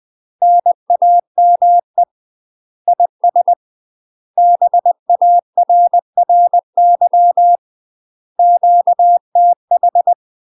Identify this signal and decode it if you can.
Morse code, heard as a keyed tone, about 20 wpm, 700 Hz. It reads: NAME IS BARRY QTH
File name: UK MASTER QSOs TRAINING / 20WPM / UK MASTER QSOs 0041